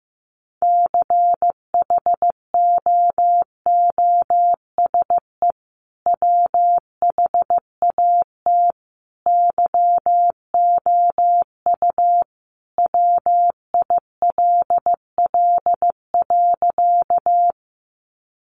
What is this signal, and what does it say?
Morse code, heard as a keyed tone, about 15 wpm, 700 Hz. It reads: CHOOSE WHAT YOU WILL.
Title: Sentences from Top 700 Words 15wpm